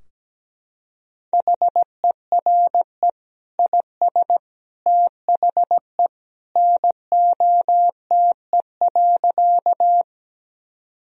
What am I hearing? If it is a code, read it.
Morse code, heard as a keyed tone, about 17 wpm, 700 Hz. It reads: HERE IS THE NOTE.
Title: Sentences from Top 1000 Words - Review Part 2 - 17wpm